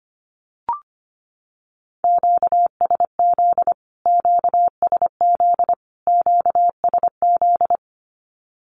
Morse code, heard as a keyed tone, about 25 wpm, 700 Hz. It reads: QSZ QSZ QSZ
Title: Q-Codes Repeated 3x 25wpm